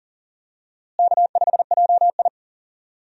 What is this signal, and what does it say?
Morse code, heard as a keyed tone, about 40 wpm, 700 Hz. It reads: K5JI
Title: USA Call Signs - 4 Characters - Repeated 3x - 40wpm